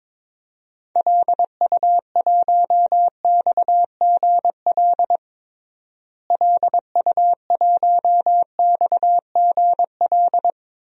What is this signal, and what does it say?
Morse code, heard as a keyed tone, about 22 wpm, 700 Hz. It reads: LU1XGL LU1XGL
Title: DXCC-100 Call Signs - 6 Characters - Repeated 3x - 22wpm